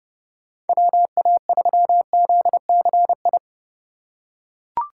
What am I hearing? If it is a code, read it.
Morse code, heard as a keyed tone, about 32 wpm, 700 Hz. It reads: WA3ZCI
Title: USA Call Signs - 4x Character Spacing - 30wpm